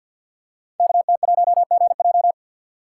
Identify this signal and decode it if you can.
Morse code, heard as a keyed tone, about 52 wpm, 700 Hz. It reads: KT1GJ
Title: Code Groups - 5 Letter:Numbers - 50wpm